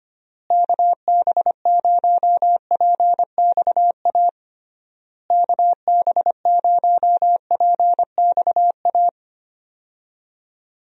Morse code, heard as a keyed tone, about 25 wpm, 700 Hz. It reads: KB0PXA KB0PXA
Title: USA Call Signs - 6 Characters - Repeated 3x - 25wpm